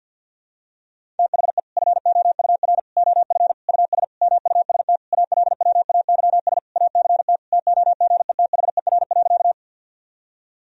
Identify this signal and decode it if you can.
Morse code, heard as a keyed tone, about 51 wpm, 700 Hz. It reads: THE FOUR OF US MUST ALWAYS ACT TOGETHER.
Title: Sentences from Top 1000 Words - Review Part 1 - 50wpm